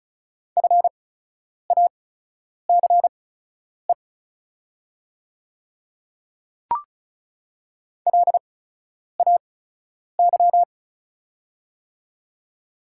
Morse code, heard as a keyed tone, about 36 wpm, spaced as for 10 wpm, 700 Hz. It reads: FACE LAY
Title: 8x Character Spacing - Top 500 Words - 35wpm